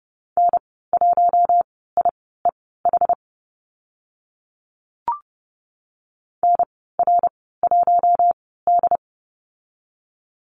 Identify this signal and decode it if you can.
Morse code, heard as a keyed tone, about 30 wpm, spaced as for 17 wpm, 700 Hz. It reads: N1IEH NR1D
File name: Lesson 14g - Number 2 - Call Signs 30wpm